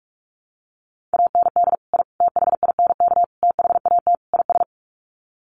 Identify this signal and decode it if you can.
Morse code, heard as a keyed tone, about 44 wpm, 700 Hz. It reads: AND I THINK THAT IS
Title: Sets of 5 Words Encore2 45wpm